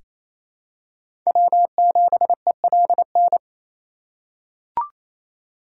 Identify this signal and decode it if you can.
Morse code, heard as a keyed tone, about 28 wpm, 700 Hz. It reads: W7ELN
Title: USA Call Signs - 5 Characters - 4x Character Spacing - 28wpm